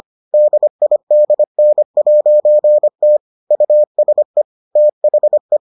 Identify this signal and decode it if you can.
Morse code, heard as a keyed tone, about 25 wpm, 600 Hz. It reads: DIDN'T USE THE